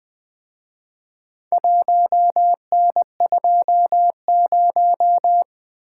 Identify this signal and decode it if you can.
Morse code, heard as a keyed tone, about 23 wpm, 700 Hz. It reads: 1N20
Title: Speed-Racing - Four Letter-Number Combinations - 20wpm